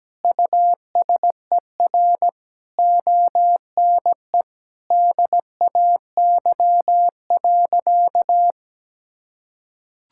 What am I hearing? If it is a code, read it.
Morse code, heard as a keyed tone, about 17 wpm, 700 Hz. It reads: USER ONE DAY.
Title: Sentences from Top 1000 Words - Review Part 5 - 17wpm